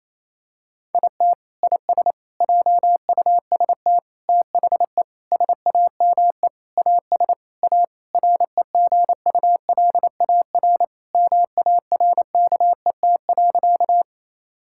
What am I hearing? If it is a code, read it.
Morse code, heard as a keyed tone, about 28 wpm, 700 Hz. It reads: IT IS JUST THE SAME AS A REGULAR MARKET.
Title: Sentences from Top 1000 Words - Review Part 6 - 28wpm